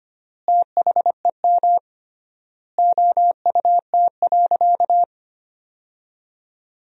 Morse code, heard as a keyed tone, about 25 wpm, 700 Hz. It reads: THEM OUT.
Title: 3x Word-Spacing - Sentences from Top 500 Words - 25wpm